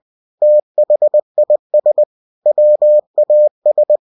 Morse code, heard as a keyed tone, about 20 wpm, 600 Hz. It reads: THIS WAS
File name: cw-books / winnie-the-pooh / winnie-20wpm-page-0108